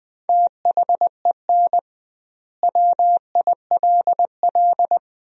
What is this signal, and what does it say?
Morse code, heard as a keyed tone, about 21 wpm, 700 Hz. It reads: THEN WILL
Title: 2x Word-Spacing - Sentences from Top 200 Words - 20wpm